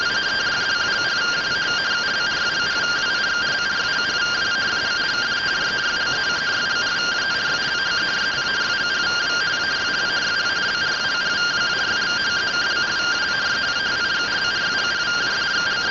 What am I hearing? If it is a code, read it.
Two-tone teleprinter signal, about 170 Hz shift.
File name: findings / CT 1 Scout 0002.gen-03 rtty rx